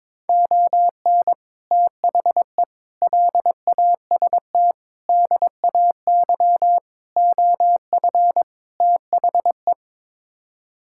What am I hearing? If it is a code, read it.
Morse code, heard as a keyed tone, about 22 wpm, 700 Hz. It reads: ON THE LAST DAY OF THE
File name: practice / 22wpm / Sets of 6 Words - 22wpm